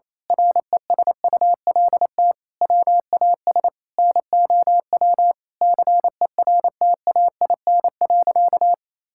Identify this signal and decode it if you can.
Morse code, heard as a keyed tone, about 27 wpm, 700 Hz. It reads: RESULT WAS NOW CERTAIN.
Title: Sentences from Top 1000 Words - Review Part 5 - 28wpm